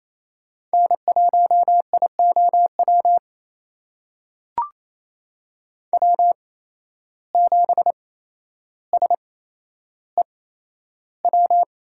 Morse code, heard as a keyed tone, about 28 wpm, spaced as for 12 wpm, 700 Hz. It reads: N1IOW W7SEW
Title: USA Call Signs - 5 Characters - 8x Character Spacing - 28wpm